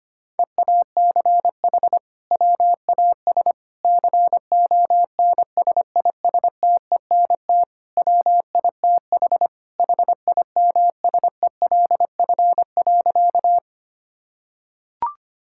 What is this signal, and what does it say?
Morse code, heard as a keyed tone, about 25 wpm, 700 Hz. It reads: EACH WAS CONSISTENT WITH HIMSELF.
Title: Sentences from Top 1600 Words 25wpm